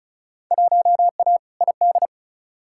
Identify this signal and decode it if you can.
Morse code, heard as a keyed tone, about 36 wpm, 700 Hz. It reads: 1A ID
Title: Field Day Exchanges - 35wpm